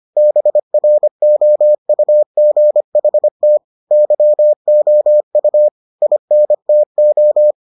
Morse code, heard as a keyed tone, about 25 wpm, 600 Hz. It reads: BROUGHT YOU INTO